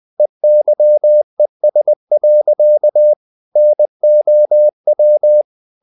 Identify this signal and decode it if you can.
Morse code, heard as a keyed tone, about 20 wpm, 600 Hz. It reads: EYES. NOW
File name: the old man and the sea / the old man and the sea part 1 20wpm